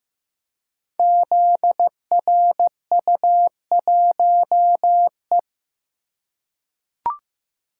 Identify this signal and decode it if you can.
Morse code, heard as a keyed tone, about 15 wpm, 700 Hz. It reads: ZRU1E
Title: Code Groups - 5 Letter:Numbers - 15wpm